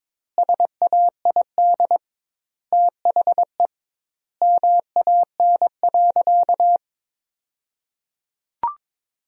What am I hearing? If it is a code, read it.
Morse code, heard as a keyed tone, about 22 wpm, 700 Hz. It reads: SAID THE MAN.
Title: 2x Word-Spacing - Sentences from Top 200 Words - 22wpm